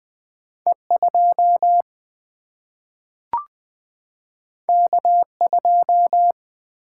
Morse code, heard as a keyed tone, about 20 wpm, 700 Hz. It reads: E2 K2
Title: Single-Letter Single-Number 20wpm